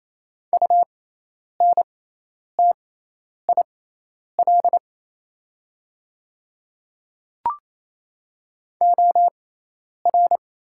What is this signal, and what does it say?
Morse code, heard as a keyed tone, about 28 wpm, spaced as for 10 wpm, 700 Hz. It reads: UNTIL OR
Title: 6x Character Spacing - Top 500 Words - 28wpm